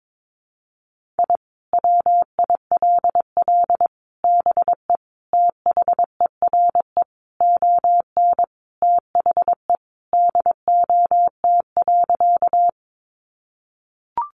Morse code, heard as a keyed tone, about 22 wpm, 700 Hz. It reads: I WILL BE THERE ON THE DOT.
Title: Sentences from Top 1500 Words 22wpm